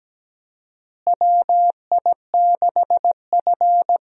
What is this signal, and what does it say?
Morse code, heard as a keyed tone, about 17 wpm, 700 Hz. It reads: WI6F
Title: USA Call Signs - 4 Characters - Repeated 2x - 17wpm